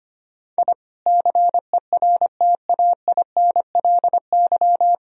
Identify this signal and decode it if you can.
Morse code, heard as a keyed tone, about 25 wpm, 700 Hz. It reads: I CERTAINLY
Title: Sentences from Top 1000 Words - Review Part 4 - 25wpm